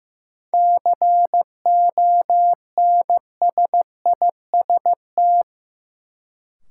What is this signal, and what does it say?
Morse code, heard as a keyed tone, about 15 wpm, 700 Hz. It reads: CONSIST
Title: Sentences from Top 1400 Words - New Word Drill Alt - 15wpm